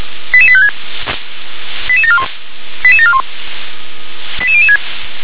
Five-tone selective call (ZVEI), heard as quick runs of five tones, tones of about 70 ms. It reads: 88856 88741 88741 88006
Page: SDR/ZVEI